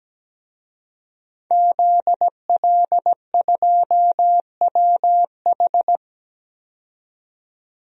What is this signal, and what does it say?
Morse code, heard as a keyed tone, about 17 wpm, 700 Hz. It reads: ZL2WH
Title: DXCC-100 Call Signs - 5 Characters - 17wpm